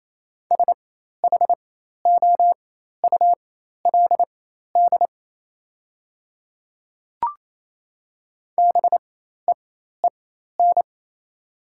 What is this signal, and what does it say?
Morse code, heard as a keyed tone, about 27 wpm, spaced as for 13 wpm, 700 Hz. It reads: SHOULD BEEN
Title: 4x Character Spacing - Top 300 Words - 28wpm